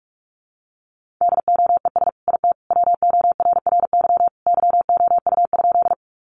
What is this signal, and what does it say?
Morse code, heard as a keyed tone, about 44 wpm, 700 Hz. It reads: DOES IT WORRY YOU?
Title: Sentences from Top 1100 Words 45wpm